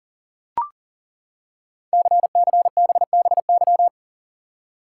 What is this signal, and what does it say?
Morse code, heard as a keyed tone, about 42 wpm, 690 Hz. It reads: CCBBY
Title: Code Groups - 5 Letters - 40wpm